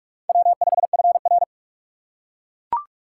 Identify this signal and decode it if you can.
Morse code, heard as a keyed tone, about 44 wpm, 700 Hz. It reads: W5FR
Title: USA Call Signs - 4 Characters - 3x Character Spacing - 45wpm